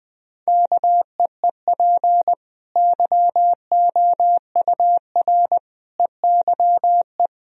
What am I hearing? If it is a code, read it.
Morse code, heard as a keyed tone, about 20 wpm, 700 Hz. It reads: KEEP YOUR EYE
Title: Sentences from Top 1000 Words - Review Part 1 - 20wpm